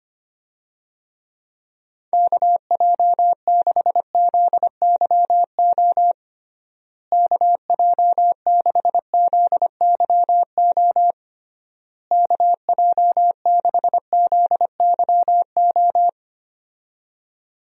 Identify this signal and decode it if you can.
Morse code, heard as a keyed tone, about 25 wpm, 700 Hz. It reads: KJ6ZYO KJ6ZYO KJ6ZYO
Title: USA Call Signs - 6 Characters - Repeated 3x - 25wpm